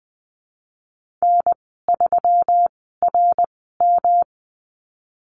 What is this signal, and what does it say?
Morse code, heard as a keyed tone, about 19 wpm, spaced as for 14 wpm, 700 Hz. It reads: N3RM
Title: 154 Lesson 25g - Letter p - Call Signs 20wpm